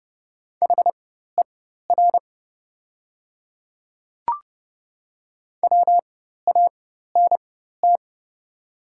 Morse code, heard as a keyed tone, about 30 wpm, spaced as for 14 wpm, 700 Hz. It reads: HER WANT
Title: 4x Character Spacing - Top 200 Words - 30wpm